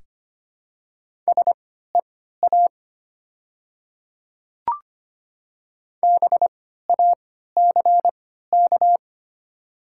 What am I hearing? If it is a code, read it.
Morse code, heard as a keyed tone, about 25 wpm, spaced as for 14 wpm, 700 Hz. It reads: SEA BACK
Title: 3x Character Spacing - Top 300 Words - 25wpm